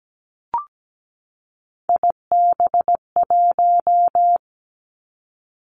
Morse code, heard as a keyed tone, about 17 wpm, 700 Hz. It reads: IB1
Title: Three Challenging Letter-Number Combinations 17wpm